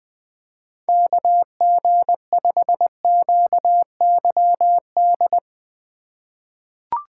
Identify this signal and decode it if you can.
Morse code, heard as a keyed tone, about 20 wpm, 700 Hz. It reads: KG5QYD
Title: USA Call Signs - 6 Characters - 2x Character Spacing - 20wpm